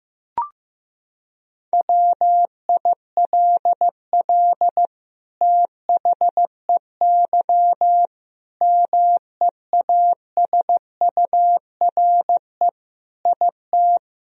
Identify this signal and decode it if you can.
Morse code, heard as a keyed tone, about 15 wpm, 700 Hz. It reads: WILL THEY MEASURE IT
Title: Sentences from Top 1000 Words - Review Part 2 - 15wpm